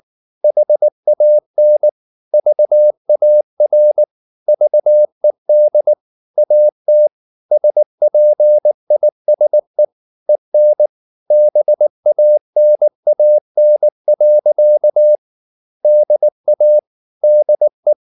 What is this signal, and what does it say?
Morse code, heard as a keyed tone, about 19 wpm, 600 Hz. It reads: HAN VAR VED AT SPISE EN BANAN. DA DE
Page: Ferie 19 wpm